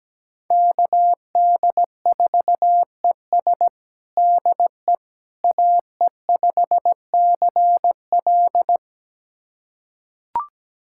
Morse code, heard as a keyed tone, about 17 wpm, 700 Hz. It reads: KD4ES DE AE5CL
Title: USA Callsign DE Callsign - 5 Characters - 17wpm